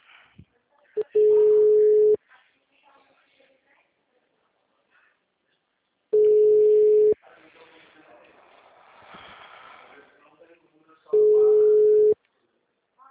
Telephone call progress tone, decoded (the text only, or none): ringback tone